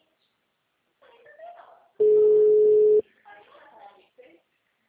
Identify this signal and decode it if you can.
Ringback tone (Europe, 425 Hz), with speech.